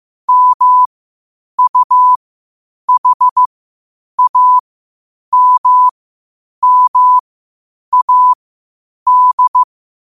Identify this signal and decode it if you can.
Morse code, heard as a keyed tone, about 15 wpm, spaced as for 8 wpm, 1000 Hz. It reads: MUHAMMAD
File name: qotdfiles / 8wpm